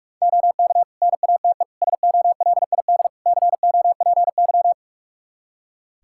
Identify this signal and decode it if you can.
Morse code, heard as a keyed tone, about 46 wpm, 700 Hz. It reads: OK NATE SOLID COPY
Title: Practice QSO - Exchange 3 - More Info - 45wpm